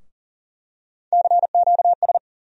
Morse code, heard as a keyed tone, about 40 wpm, 700 Hz. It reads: CQS
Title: USA Call Sign Postfixes - 3 Letters 40wpm